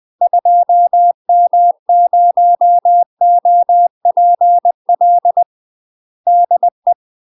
Morse code, heard as a keyed tone, about 20 wpm, 700 Hz. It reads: 2M0OPL DE
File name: UK MASTER QSOs TRAINING / 20WPM / UK MASTER QSOs 0015